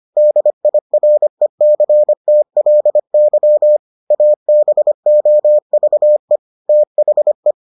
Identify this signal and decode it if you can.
Morse code, heard as a keyed tone, about 25 wpm, 600 Hz. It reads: DIRECTLY ABOVE THE